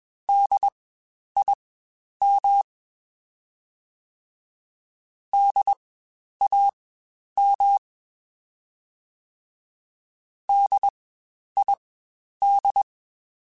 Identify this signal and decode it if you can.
Morse code, heard as a keyed tone, about 21 wpm, spaced as for 10 wpm, 800 Hz. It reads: DIM DAM DID
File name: Spaced Alphas / Spaced Alphas mp3s / Ds sp @20